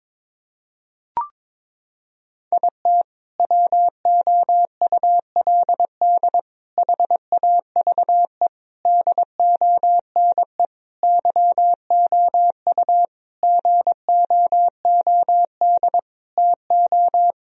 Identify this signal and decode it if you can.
Morse code, heard as a keyed tone, about 22 wpm, 700 Hz. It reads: IT WOULD HAVE DONE YOU GOOD TO
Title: Sentences from Top 1000 Words - Review Part 2 - 22wpm